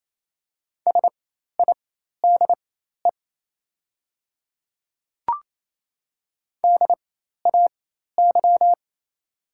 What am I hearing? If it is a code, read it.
Morse code, heard as a keyed tone, about 28 wpm, spaced as for 13 wpm, 700 Hz. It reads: SIDE DAY
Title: 4x Character Spacing - Top 100 Words - 28wpm